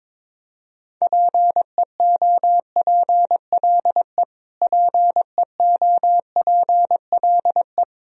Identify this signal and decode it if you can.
Morse code, heard as a keyed tone, about 22 wpm, 700 Hz. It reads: PEOPLE PEOPLE
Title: Top 100 Words Repeated 22wpm